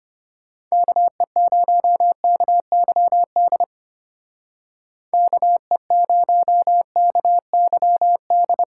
Morse code, heard as a keyed tone, about 32 wpm, 700 Hz. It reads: KE0KYD KE0KYD